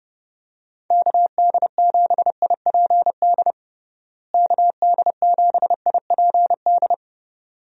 Morse code, heard as a keyed tone, about 30 wpm, 700 Hz. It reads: KD7IPD KD7IPD
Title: USA Call Signs - 6 Characters - Repeated 2x - 30wpm